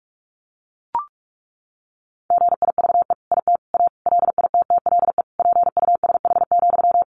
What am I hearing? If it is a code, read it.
Morse code, heard as a keyed tone, about 44 wpm, 700 Hz. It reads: GIVE IT A LITTLE PUSH,
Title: Sentences from Top 1500 Words 45wpm